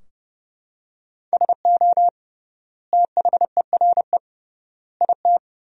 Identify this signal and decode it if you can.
Morse code, heard as a keyed tone, about 30 wpm, 700 Hz. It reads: SO THERE IT
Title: 3x Word-Spacing - Sentences from Top 100 Words - 30wpm